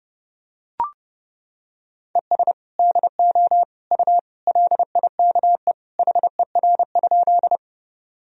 Morse code, heard as a keyed tone, about 30 wpm, 700 Hz. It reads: ES DO U LIKE HER?
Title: Sentences from Top 100 Words and Abbreviations 30wpm